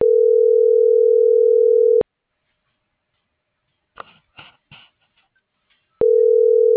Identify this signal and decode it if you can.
Ringback tone (North America), 440 + 480 Hz.